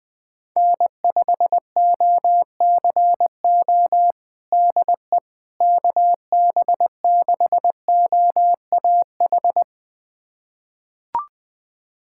Morse code, heard as a keyed tone, about 20 wpm, 700 Hz. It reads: N5OCO DE KB6OAH
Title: USA Callsign DE Callsign 20wpm